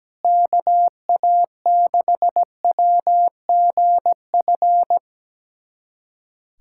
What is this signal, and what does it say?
Morse code, heard as a keyed tone, about 17 wpm, 700 Hz. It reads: KA6WGF
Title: USA Callsign DE Callsign 17wpm